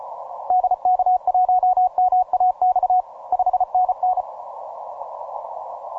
Morse code, heard as a keyed tone, about 34 wpm, 740 Hz. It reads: DK1MAX 5NN
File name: simple-file-list / MM0UKI EU189 ROCKALL 20MCW